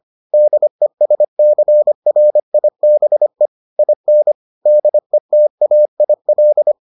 Morse code, heard as a keyed tone, about 25 wpm, 600 Hz. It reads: DESCRIBE IN DETAIL